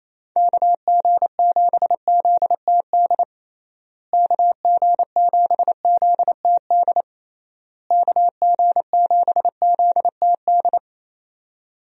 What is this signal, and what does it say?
Morse code, heard as a keyed tone, about 28 wpm, 700 Hz. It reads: KG7ZTD KG7ZTD KG7ZTD
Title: USA Call Signs - 6 Characters - Repeated 3x - 28wpm